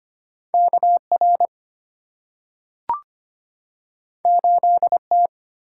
Morse code, heard as a keyed tone, about 25 wpm, 700 Hz. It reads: KR 8T
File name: Two Letter-Number Combinations - 25wpm